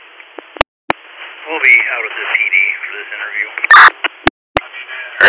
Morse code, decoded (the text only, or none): none